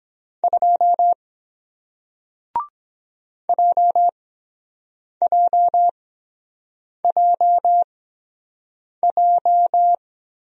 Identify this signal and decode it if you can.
Morse code, heard as a keyed tone, about 23 wpm, spaced as for 7 wpm, 700 Hz. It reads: 2 JJJJ